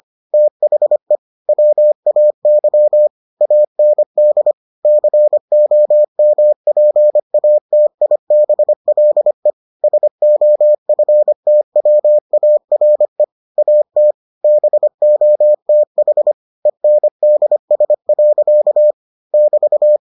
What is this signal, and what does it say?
Morse code, heard as a keyed tone, about 25 wpm, 600 Hz. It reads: THE WAY AND COMPATIBLE SOFTWARE AT BOTH ENDS. <BT>